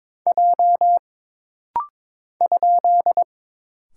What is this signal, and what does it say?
Morse code, heard as a keyed tone, about 22 wpm, 700 Hz. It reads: J ?
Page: This rapid-fire timing is geared to push your ICR (Instant Character Recognition) skills.